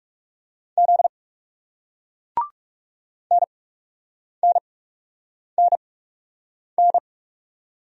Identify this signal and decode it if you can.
Morse code, heard as a keyed tone, about 39 wpm, spaced as for 8 wpm, 700 Hz. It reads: Z NNNN